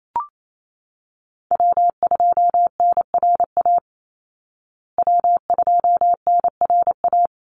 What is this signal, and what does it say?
Morse code, heard as a keyed tone, about 28 wpm, 700 Hz. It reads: W2NRA W2NRA
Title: Call Signs Repeated 3x - 28wpm